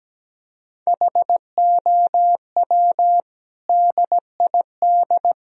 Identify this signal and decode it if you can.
Morse code, heard as a keyed tone, about 17 wpm, 700 Hz. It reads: HOW DID
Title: Sentences from Top 1000 Words - Review Part 1 - 17wpm